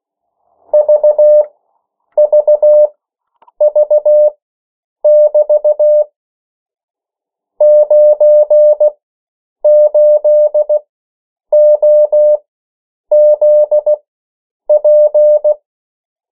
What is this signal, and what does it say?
Morse code, heard as a keyed tone, about 16 wpm, spaced as for 8 wpm, 600 Hz. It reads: VVV<BT> 98OZP
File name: morse mp3 / 14 urok 13